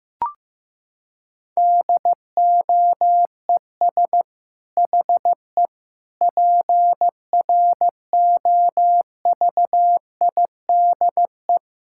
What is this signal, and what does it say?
Morse code, heard as a keyed tone, about 15 wpm, 700 Hz. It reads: DOES HE PROVIDE
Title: Sentences from Top 1000 Words - Review Part 6 - 15wpm